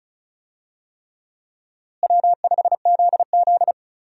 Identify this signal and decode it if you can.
Morse code, heard as a keyed tone, about 36 wpm, 700 Hz. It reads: W5ZZ
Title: USA Call Signs - 4 Characters - Repeated 3x - 35wpm